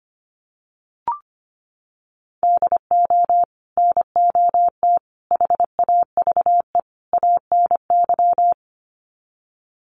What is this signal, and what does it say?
Morse code, heard as a keyed tone, about 25 wpm, 700 Hz. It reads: DO NOT HAVE ANY
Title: Sets of 4 Words 25wpm